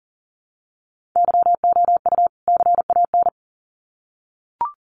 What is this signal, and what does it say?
Morse code, heard as a keyed tone, about 40 wpm, 700 Hz. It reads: YOU CAN
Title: Sets of 2 Words 40wpm